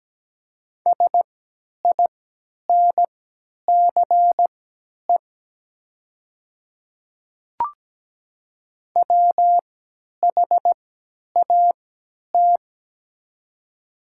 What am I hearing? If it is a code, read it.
Morse code, heard as a keyed tone, about 17 wpm, spaced as for 10 wpm, 700 Hz. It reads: SINCE WHAT
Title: 3x Character Spacing - Top 300 Words - 17wpm